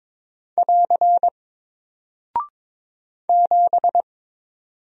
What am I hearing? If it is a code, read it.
Morse code, heard as a keyed tone, about 22 wpm, 700 Hz. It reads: <AR> 7
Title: Single Character (Letter:Number:Punctuation:Prosign) Alt 22wpm